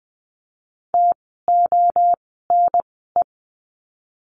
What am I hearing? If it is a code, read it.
Morse code, heard as a keyed tone, about 20 wpm, spaced as for 15 wpm, 700 Hz. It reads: TONE
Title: 068 Lesson 12f - Letter d - Words 20wpm